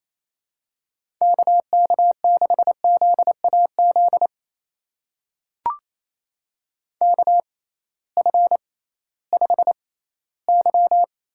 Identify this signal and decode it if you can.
Morse code, heard as a keyed tone, about 28 wpm, spaced as for 16 wpm, 700 Hz. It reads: KK6ZAZ KF5Y
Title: USA Call Signs - 6 Characters - 6x Character Spacing - 28wpm